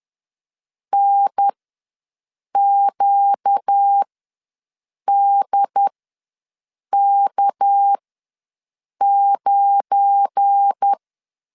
Morse code, heard as a keyed tone, about 10 wpm, spaced as for 6 wpm, 800 Hz. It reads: NQDK9